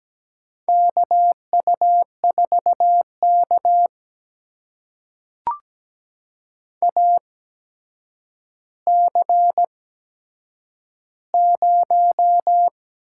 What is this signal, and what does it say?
Morse code, heard as a keyed tone, about 17 wpm, spaced as for 8 wpm, 700 Hz. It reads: KU4K AC0
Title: USA Call Signs - 4 Characters - 8x Character Spacing - 17wpm